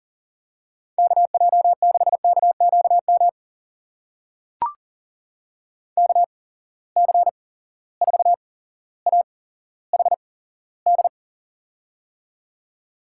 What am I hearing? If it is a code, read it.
Morse code, heard as a keyed tone, about 41 wpm, spaced as for 17 wpm, 700 Hz. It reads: KJ6KQM KC4AHD
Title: USA Call Signs - 6 Characters - 8x Character Spacing - 40wpm